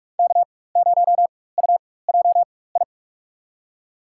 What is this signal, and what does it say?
Morse code, heard as a keyed tone, about 43 wpm, spaced as for 21 wpm, 700 Hz. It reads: K0UJI
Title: USA Call Signs - 5 Characters - 4x Character Spacing - 45wpm